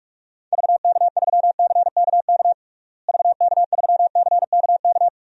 Morse code, heard as a keyed tone, about 46 wpm, 700 Hz. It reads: VK3CKK VK3CKK
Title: DXCC-100 Call Signs - 6 Characters - Repeated 3x - 45wpm